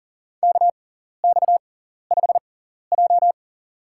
Morse code, heard as a keyed tone, about 40 wpm, spaced as for 14 wpm, 700 Hz. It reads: KX5J
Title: USA Call Signs - 4 Characters - 6x Character Spacing - 40wpm